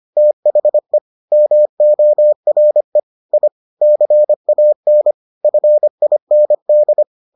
Morse code, heard as a keyed tone, about 25 wpm, 600 Hz. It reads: THE MORE I CAN FIND